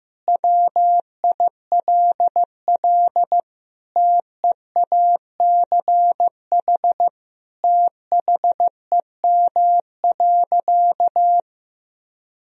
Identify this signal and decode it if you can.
Morse code, heard as a keyed tone, about 16 wpm, 700 Hz. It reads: WILL TEACH THEM.
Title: Sentences from Top 1000 Words - Review Part 2 - 15wpm